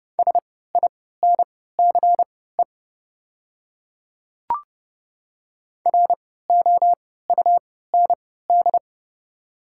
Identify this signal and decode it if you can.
Morse code, heard as a keyed tone, about 30 wpm, spaced as for 17 wpm, 690 Hz. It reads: SINCE ROUND
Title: 3x Character Spacing - Top 300 Words - 30wpm